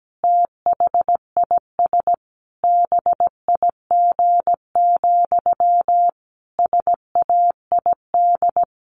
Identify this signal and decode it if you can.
Morse code, heard as a keyed tone, about 17 wpm, 700 Hz. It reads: THIS BIG, SAID
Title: A Sound Of Thunder 17wpm